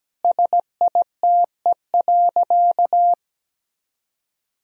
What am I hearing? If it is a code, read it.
Morse code, heard as a keyed tone, about 17 wpm, 700 Hz. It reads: SITE.
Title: Sentences from Top 1000 Words - Review Part 4 - 17wpm